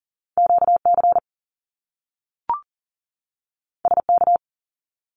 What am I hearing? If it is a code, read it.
Morse code, heard as a keyed tone, about 40 wpm, 700 Hz. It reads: QC SK
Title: Canadian Provinces and Territories 40wpm